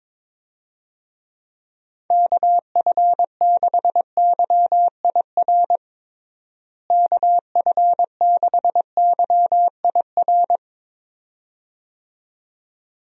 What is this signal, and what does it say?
Morse code, heard as a keyed tone, about 22 wpm, 700 Hz. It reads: KF6YIR KF6YIR
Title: USA Call Signs - 6 Characters - Repeated 2x - 22wpm